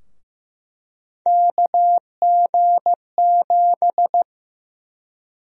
Morse code, heard as a keyed tone, about 15 wpm, 700 Hz. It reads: KG7
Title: USA Call Sign Prefixes - Repeated 2x - 15wpm